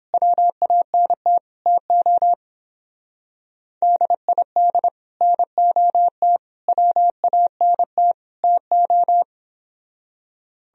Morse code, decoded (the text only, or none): WANT TO DID NOT WANT TO